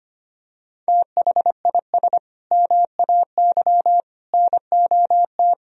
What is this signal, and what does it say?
Morse code, heard as a keyed tone, about 25 wpm, 700 Hz. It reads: THIS MAY NOT
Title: Sentences from Top 1000 Words - Review Part 3 - 25wpm